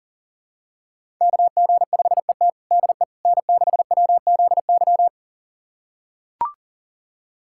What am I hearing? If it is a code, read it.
Morse code, heard as a keyed tone, about 37 wpm, 700 Hz. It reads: KG5ET DE N6WZY
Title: USA Callsign DE Callsign - 5 Characters - 40wpm